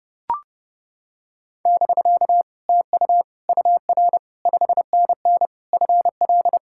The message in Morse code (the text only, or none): <BK> TU UR 5NN FL